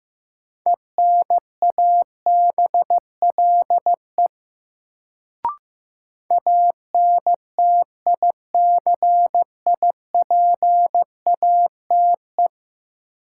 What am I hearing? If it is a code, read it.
Morse code, heard as a keyed tone, about 16 wpm, 700 Hz. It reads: ENABLE ANTICIPATE
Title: Sentences from Top 1500 Words - New Word Drill - 15wpm